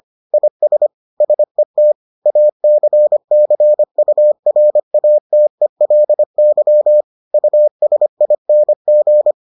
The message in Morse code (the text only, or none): IS SET ACCURATELY USING